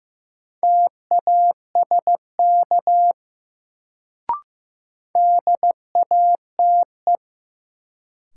Morse code, heard as a keyed tone, about 15 wpm, 700 Hz. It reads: TASK DATE
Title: Sentences from Top 1100 Words - New Word Drill Alt - 15wpm